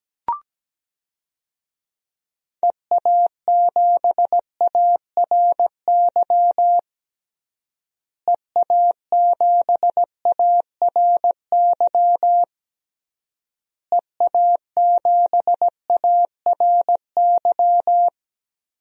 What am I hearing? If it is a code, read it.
Morse code, heard as a keyed tone, about 17 wpm, 700 Hz. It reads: EA7ARY EA7ARY EA7ARY
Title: DXCC-100 Call Signs - 6 Characters - Repeated 3x - 17wpm